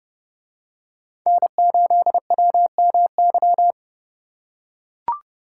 Morse code, heard as a keyed tone, about 30 wpm, 700 Hz. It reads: N8WMY